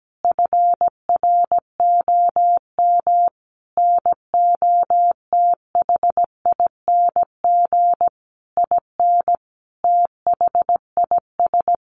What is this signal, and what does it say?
Morse code, heard as a keyed tone, about 17 wpm, 700 Hz. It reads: FROM NOTHING IN THIS
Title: Sentences from Top 1200 Words 17wpm